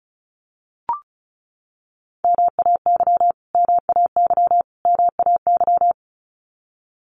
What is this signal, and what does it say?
Morse code, heard as a keyed tone, about 34 wpm, 700 Hz. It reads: MAY MAY MAY
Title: Top 200 Words Repeated 35wpm